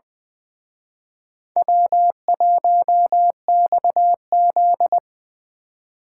Morse code, heard as a keyed tone, about 23 wpm, 700 Hz. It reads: W1XZ
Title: Speed Racing - USA 1x2 Call Signs 20wpm